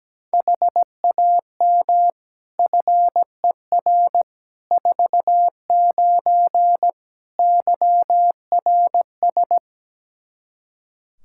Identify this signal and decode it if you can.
Morse code, heard as a keyed tone, about 17 wpm, 700 Hz. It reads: HAM FER 49 YRS
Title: K1USN SST QSO Practice 17wpm